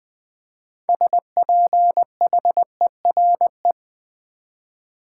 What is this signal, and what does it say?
Morse code, heard as a keyed tone, about 20 wpm, 700 Hz. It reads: SPHERE
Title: 6 Letter Words - First 3 Letters Given - 20wpm